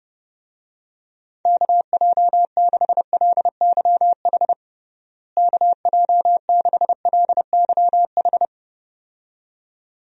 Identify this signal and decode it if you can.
Morse code, heard as a keyed tone, about 30 wpm, 700 Hz. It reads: KJ6LYH KJ6LYH
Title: USA Call Signs - 6 Characters - Repeated 2x - 30wpm